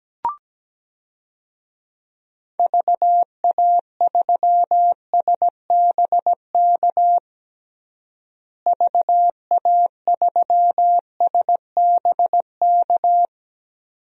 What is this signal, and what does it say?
Morse code, heard as a keyed tone, about 17 wpm, 700 Hz. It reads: VA3SBK VA3SBK
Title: DXCC-100 Call Signs - 6 Characters - Repeated 2x - 17wpm